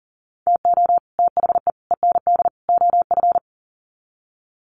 Morse code, heard as a keyed tone, about 40 wpm, 700 Hz. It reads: TO THE END OF
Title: Sets of 4 Words 40wpm